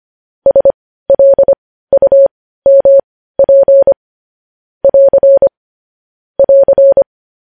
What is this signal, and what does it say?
Morse code, heard as a keyed tone, about 25 wpm, spaced as for 15 wpm, 550 Hz. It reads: SLUMP <AR> <AR>
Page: news headlines in morse code at 15 wpm